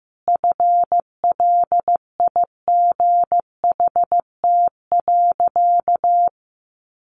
Morse code, heard as a keyed tone, about 15 wpm, 700 Hz. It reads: FLIGHT.
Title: Sentences from Top 1400 Words 15wpm